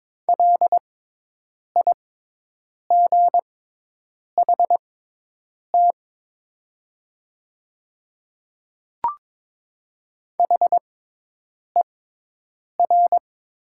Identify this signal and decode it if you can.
Morse code, heard as a keyed tone, about 22 wpm, spaced as for 8 wpm, 700 Hz. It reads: LIGHT HER
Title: 6x Character Spacing - Top 200 Words - 22wpm